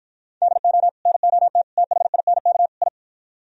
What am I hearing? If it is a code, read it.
Morse code, heard as a keyed tone, about 52 wpm, 700 Hz. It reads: DO NOT THINK I